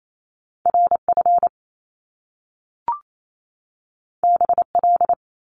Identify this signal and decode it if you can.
Morse code, heard as a keyed tone, about 28 wpm, 700 Hz. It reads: RF BL
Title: Top Two Letter Combinations in Words 28wpm